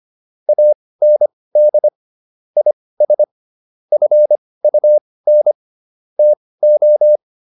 Morse code, heard as a keyed tone, about 25 wpm, spaced as for 18 wpm, 600 Hz. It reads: AND IS FUN TO